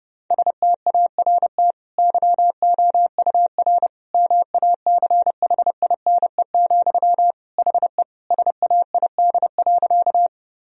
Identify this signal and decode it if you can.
Morse code, heard as a keyed tone, about 30 wpm, 700 Hz. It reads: START YOUR MACHINE, HE SAID.
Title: Sentences from Top 1000 Words - Review Part 2 - 30wpm